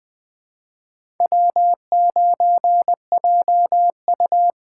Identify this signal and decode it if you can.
Morse code, heard as a keyed tone, about 20 wpm, 700 Hz. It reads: W9JU
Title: USA Call Signs - 4 Characters - 3x Character Spacing - 20wpm